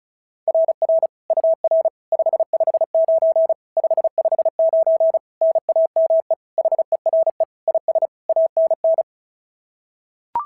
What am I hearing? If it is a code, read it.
Morse code, heard as a keyed tone, about 36 wpm, 690 Hz. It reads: RR UR 559 559 NAME HERE IS ANN
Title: SOTA QSO Practice 35wpm